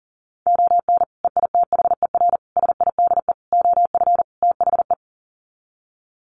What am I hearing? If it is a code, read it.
Morse code, heard as a keyed tone, about 40 wpm, 700 Hz. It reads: ON EITHER SIDE OF THE
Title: Sets of 5 Words 40wpm